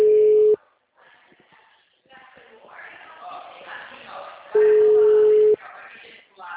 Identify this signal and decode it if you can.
Ringback tone (Europe, 425 Hz), with speech.
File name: out- -2009- - - .48